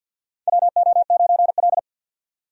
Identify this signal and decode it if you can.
Morse code, heard as a keyed tone, about 50 wpm, 700 Hz. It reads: WO9L
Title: USA Call Signs - 4 Characters - 50wpm